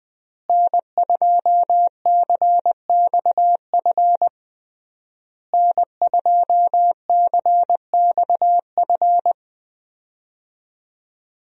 Morse code, heard as a keyed tone, about 20 wpm, 700 Hz. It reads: N2CXF N2CXF
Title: USA Call Signs - 5 Characters - Repeated 3x - 20wpm